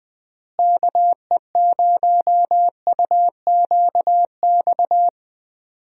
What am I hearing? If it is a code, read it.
Morse code, heard as a keyed tone, about 21 wpm, 700 Hz. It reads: KE0UQX
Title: USA Call Signs - 6 Characters - Repeated 3x - 20wpm